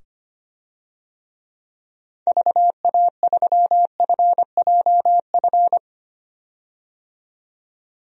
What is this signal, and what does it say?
Morse code, heard as a keyed tone, about 25 wpm, 700 Hz. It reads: VA3FJF
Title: DXCC-100 Call Signs - 6 Characters - 25wpm